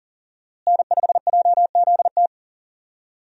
Morse code, heard as a keyed tone, about 40 wpm, 700 Hz. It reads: N5JZT
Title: USA Call Signs - 5 Characters - Repeated 2x - 40wpm